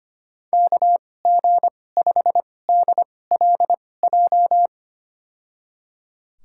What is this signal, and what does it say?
Morse code, heard as a keyed tone, about 25 wpm, spaced as for 18 wpm, 700 Hz. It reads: KG5DLJ
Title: USA Call Signs - 2x Character Spacing - 25wpm